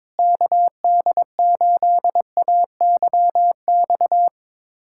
Morse code, heard as a keyed tone, about 22 wpm, 700 Hz. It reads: KD8AYX
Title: USA Callsign DE Callsign 22wpm